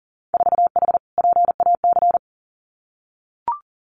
Morse code, heard as a keyed tone, about 39 wpm, 700 Hz. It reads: 4H PAC
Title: Winter Field Day Exchanges Encore 40wpm